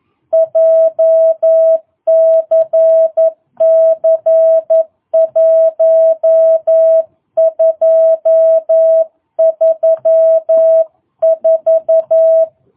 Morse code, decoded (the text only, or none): JCC 1234